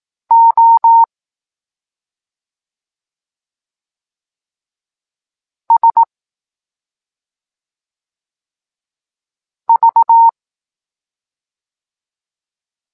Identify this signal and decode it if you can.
Morse code, heard as a keyed tone, about 18 wpm, 920 Hz. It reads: O S V
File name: Tekst-9-rownomierny